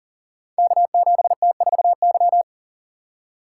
Morse code, heard as a keyed tone, about 42 wpm, 700 Hz. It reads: K7T4Y
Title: Code Groups - 5 Letter:Numbers - 40wpm